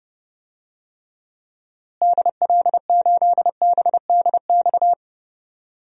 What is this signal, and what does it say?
Morse code, heard as a keyed tone, about 32 wpm, 700 Hz. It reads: DL8BDX
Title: DXCC-100 Call Signs - 6 Characters - Repeated 3x - 30wpm